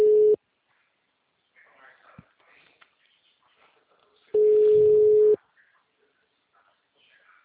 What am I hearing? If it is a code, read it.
Ringback tone (Europe, 425 Hz).
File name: out- -2019- - - .82